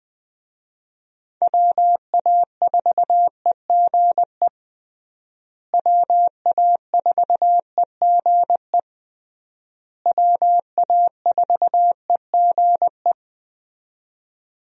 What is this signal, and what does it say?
Morse code, heard as a keyed tone, about 20 wpm, 700 Hz. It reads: WA4EGE WA4EGE WA4EGE
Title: USA Call Signs - 6 Characters - Repeated 3x - 20wpm